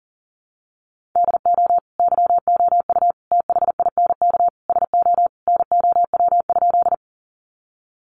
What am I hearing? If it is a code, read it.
Morse code, heard as a keyed tone, about 40 wpm, 700 Hz. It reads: DO YOU THINK SO NOW?
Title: Sentences from Top 200 Words 40wpm